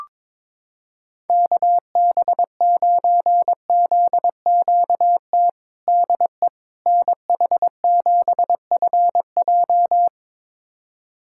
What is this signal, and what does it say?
Morse code, heard as a keyed tone, about 21 wpm, 700 Hz. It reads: KB9ZQT DE NH7FJ
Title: USA Callsign DE Callsign 22wpm